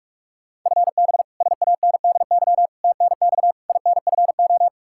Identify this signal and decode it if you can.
Morse code, heard as a keyed tone, about 46 wpm, 700 Hz. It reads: FB SANDY TNX INFO
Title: Practice QSO - Exchange 3 - More Info - 45wpm